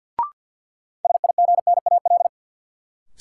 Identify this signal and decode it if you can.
Morse code, heard as a keyed tone, about 50 wpm, 700 Hz. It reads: SIGNAL
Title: Sentences from Top 1400 Words - New Word Drill - 50wpm